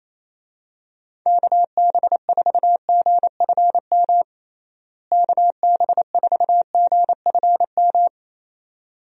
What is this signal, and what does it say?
Morse code, heard as a keyed tone, about 28 wpm, 700 Hz. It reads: KB4GFM KB4GFM
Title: USA Call Signs - 6 Characters - Repeated 2x - 28wpm